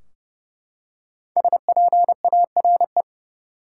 Morse code, heard as a keyed tone, about 32 wpm, 700 Hz. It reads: SPARE
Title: Five Letter Words - 30wpm